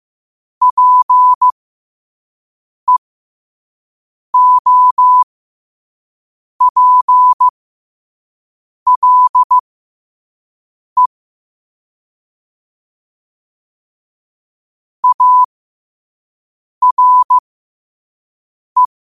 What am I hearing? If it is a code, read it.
Morse code, heard as a keyed tone, about 15 wpm, spaced as for 5 wpm, 1000 Hz. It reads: PEOPLE ARE
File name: qotdfiles / 5wpm